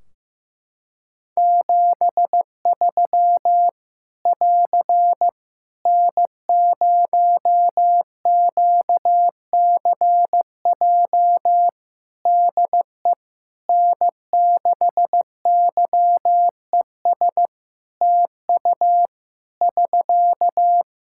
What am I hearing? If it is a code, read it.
Morse code, heard as a keyed tone, about 16 wpm, 700 Hz. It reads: 73 <AR> N0QCJ DE N6YES TU <SK>
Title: K1USN SST QSO Practice 15wpm